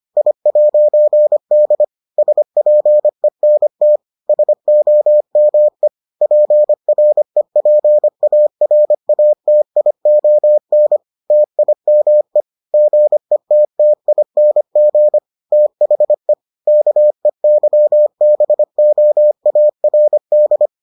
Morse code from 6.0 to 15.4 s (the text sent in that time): PREPARATION TIME GETTING